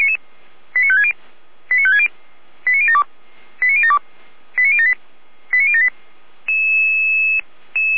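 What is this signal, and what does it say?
Five-tone selective call (ZVEI), heard as quick runs of five tones, tones of about 70 ms. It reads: 89560 89560 89072 89072 89078 89078
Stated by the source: ZVEI five-tone selective call 89170, received with an SDR.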